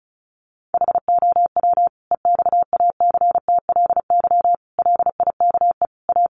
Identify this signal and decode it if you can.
Morse code, heard as a keyed tone, about 34 wpm, 700 Hz. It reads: HOW EXACTLY LIKE A
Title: Sentences from Top 900 Words 35wpm